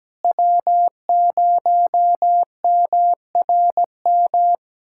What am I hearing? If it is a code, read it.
Morse code, heard as a keyed tone, about 17 wpm, 700 Hz. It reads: W0MRM
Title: USA Callsign DE Callsign - 5 Characters - 17wpm